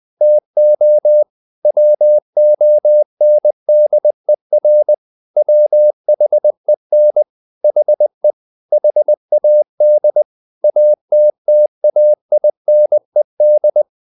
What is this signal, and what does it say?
Morse code, heard as a keyed tone, about 20 wpm, 600 Hz. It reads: TO WONDER WHEN HE HAD ATTAINED
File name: the old man and the sea / the old man and the sea part 1 20wpm